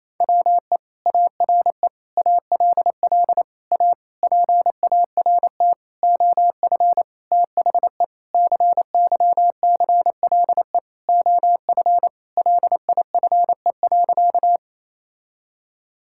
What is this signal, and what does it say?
Morse code, heard as a keyed tone, about 28 wpm, 700 Hz. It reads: WE ARE ALL A PART OF THE CYCLE OF LIFE.
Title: Sentences from Top 1000 Words - Review Part 6 - 28wpm